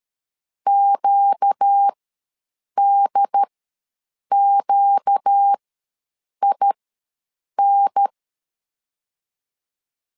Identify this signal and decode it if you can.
Morse code, heard as a keyed tone, about 13 wpm, spaced as for 7 wpm, 800 Hz. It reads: QDQIN
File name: LEZIONI / 4 Q ws 12wpm 1912